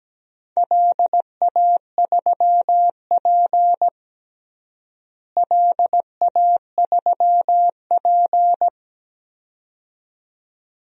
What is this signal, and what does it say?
Morse code, heard as a keyed tone, about 17 wpm, 700 Hz. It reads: LA3P LA3P
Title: DXCC-100 Call Signs - 4 Characters - Repeated 2x - 17wpm